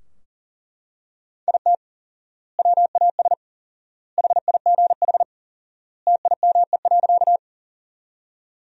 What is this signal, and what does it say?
Morse code, heard as a keyed tone, about 40 wpm, 700 Hz. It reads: IT WAS HIGH TIME.
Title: 4x Word-Spacing - Sentences from Top 200 Words - 40wpm